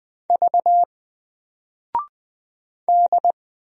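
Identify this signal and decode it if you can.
Morse code, heard as a keyed tone, about 20 wpm, 700 Hz. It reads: V D
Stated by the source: Single letter and number Morse drill at 20 wpm, from N4LCD.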